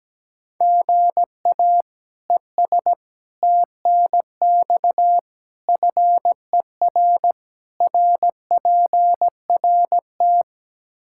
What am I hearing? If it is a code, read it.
Morse code, heard as a keyed tone, about 17 wpm, 700 Hz. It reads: GA ES TNX FER RPRT
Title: Practice QSO - Exchange 1 - Beginning - 17wpm